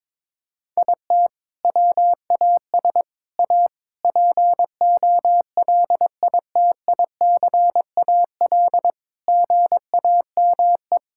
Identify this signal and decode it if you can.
Morse code, heard as a keyed tone, about 21 wpm, 700 Hz. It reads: IT WAS A POLITICAL GAME
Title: Sentences from Top 1000 Words - Review Part 5 - 22wpm